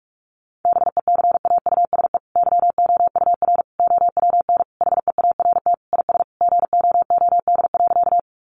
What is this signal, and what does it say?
Morse code, heard as a keyed tone, about 44 wpm, 700 Hz. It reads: BECAUSE YOUR OWN HEART IS GOOD.
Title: Sentences from Top 1200 Words 45wpm